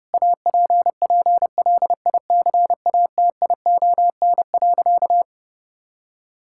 Morse code, heard as a keyed tone, about 30 wpm, 700 Hz. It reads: APPLICATION.
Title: Sentences from Top 1600 Words 30wpm